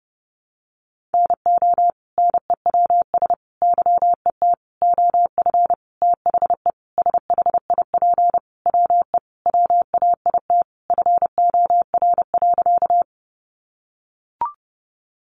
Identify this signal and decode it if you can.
Morse code, heard as a keyed tone, about 30 wpm, 700 Hz. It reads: NO NEWS YET OF THE SHIP WE WAIT FOR.
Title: Sentences from Top 800 Words 30wpm